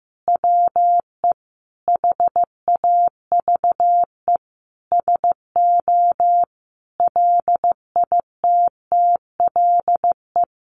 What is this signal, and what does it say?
Morse code, heard as a keyed tone, about 15 wpm, 700 Hz. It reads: WE HAVE SO LITTLE
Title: Sentences from Top 1200 Words 15wpm